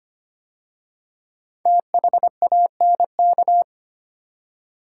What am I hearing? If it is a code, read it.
Morse code, heard as a keyed tone, about 25 wpm, 700 Hz. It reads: THANK
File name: 5 Letter Words - First 3 Letters Given - 25wpm